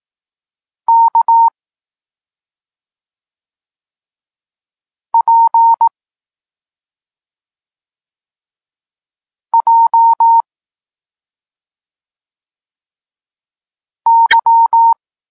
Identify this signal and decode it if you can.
Morse code, heard as a keyed tone, about 18 wpm, 920 Hz. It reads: K P J Y
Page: Tekst 32 – Doskonalenie odbioru poznanych liter